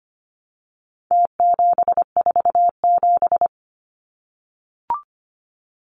Morse code, heard as a keyed tone, about 25 wpm, 700 Hz. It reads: T747
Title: Numbers - Contest Style - 4-Digit 25wpm